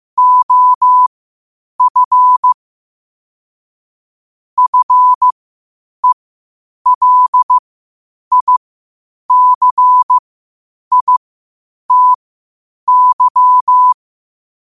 Morse code, heard as a keyed tone, about 15 wpm, spaced as for 8 wpm, 1000 Hz. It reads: OF FELICITY
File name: qotdfiles / 8wpm